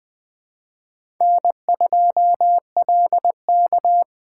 Morse code, heard as a keyed tone, about 20 wpm, 700 Hz. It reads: N2LK
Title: K1USN SST QSO Practice - 20wpm